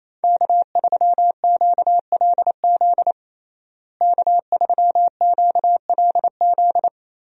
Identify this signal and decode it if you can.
Morse code, heard as a keyed tone, about 27 wpm, 700 Hz. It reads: K3QLZ K3QLZ
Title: USA Call Signs - 5 Characters - Repeated 2x - 28wpm